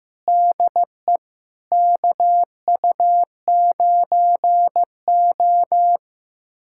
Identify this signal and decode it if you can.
Morse code, heard as a keyed tone, about 15 wpm, 700 Hz. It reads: DE KU9O
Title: USA Callsign DE Callsign 15wpm